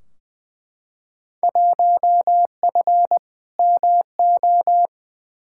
Morse code, heard as a keyed tone, about 20 wpm, 700 Hz. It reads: 1F MO
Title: Field Day Exchanges Encore - 20wpm